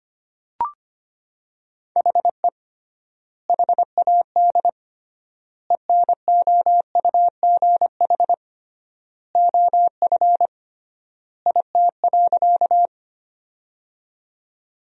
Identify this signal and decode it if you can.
Morse code, heard as a keyed tone, about 25 wpm, 700 Hz. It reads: HE HAD ENOUGH OF IT.
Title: 3x Word-Spacing - Sentences from Top 400 Words - 25wpm